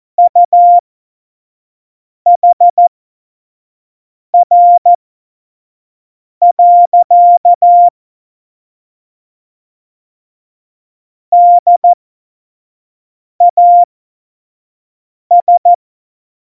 Morse code, heard as a keyed tone, about 14 wpm, spaced as for 5 wpm, 700 Hz. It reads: UHR. DAS
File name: Morsekurs / Witze / Witz 19. September 17, Tempo 25